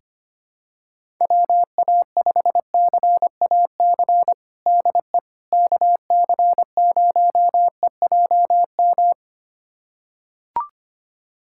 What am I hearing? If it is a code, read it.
Morse code, heard as a keyed tone, about 25 wpm, 700 Hz. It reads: WA5CAC DE KC0EJM
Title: USA Callsign DE Callsign 25wpm